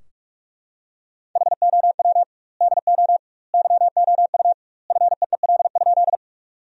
Morse code, heard as a keyed tone, about 48 wpm, 700 Hz. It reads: HOW DO YOU FEEL?
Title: 2x Word-Spacing - Sentences from Top 400 Words - 45wpm